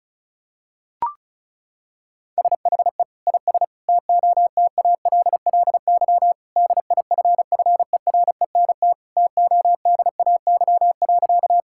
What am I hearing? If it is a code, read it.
Morse code, heard as a keyed tone, about 36 wpm, 700 Hz. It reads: SHE IS TOTALLY DIFFERENT TODAY.
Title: Sentences from Top 1600 Words 35wpm